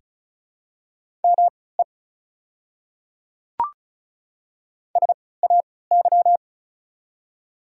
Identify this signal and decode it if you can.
Morse code, heard as a keyed tone, about 34 wpm, spaced as for 20 wpm, 700 Hz. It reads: ME SAY
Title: 3x Character Spacing - Top 300 Words - 35wpm